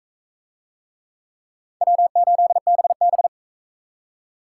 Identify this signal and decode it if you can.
Morse code, heard as a keyed tone, about 38 wpm, 700 Hz. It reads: W8BB